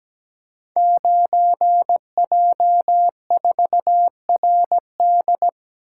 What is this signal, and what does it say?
Morse code, heard as a keyed tone, about 17 wpm, 700 Hz. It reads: 9J4RD
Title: Code Groups - 5 Letter:Numbers - 17wpm